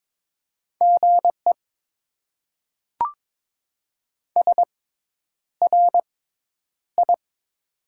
Morse code, heard as a keyed tone, about 22 wpm, spaced as for 10 wpm, 700 Hz. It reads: GE SRI
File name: Top 90 QSO Elements - 6x Character Spacing - 22wpm